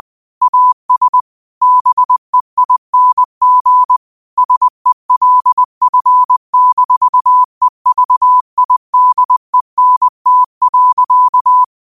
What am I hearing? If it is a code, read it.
Morse code, heard as a keyed tone, about 20 wpm, 1000 Hz. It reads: AS BEING SELF-EVIDENT.
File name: qotdfiles / 20wpm